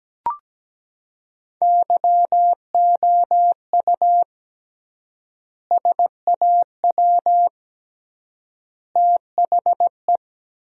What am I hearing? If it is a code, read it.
Morse code, heard as a keyed tone, about 17 wpm, 700 Hz. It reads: YOU SAW THE
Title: 3x Word-Spacing - Sentences from Top 400 Words - 17wpm